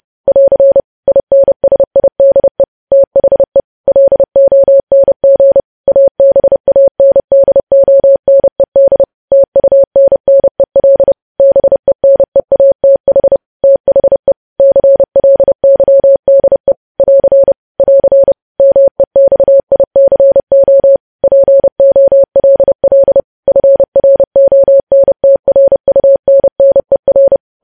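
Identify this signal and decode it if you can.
Morse code, heard as a keyed tone, about 30 wpm, 550 Hz. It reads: <AR> INSIDE THE LONG ABANDONED TUNNEL BENEATH THE CLYDE <AR> <AR> MEXICO POLL FRONTRUNNER
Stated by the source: Morse news headlines sent at 30 wpm, opening with vvv.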